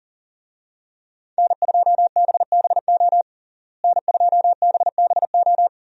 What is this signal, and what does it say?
Morse code, heard as a keyed tone, about 40 wpm, 700 Hz. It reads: N2BBO N2BBO
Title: USA Call Signs - 5 Characters - Repeated 2x - 40wpm